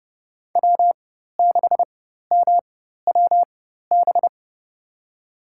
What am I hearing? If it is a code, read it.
Morse code, heard as a keyed tone, about 30 wpm, spaced as for 14 wpm, 700 Hz. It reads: W6MWB
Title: USA Call Signs - 5 Characters - 4x Character Spacing - 30wpm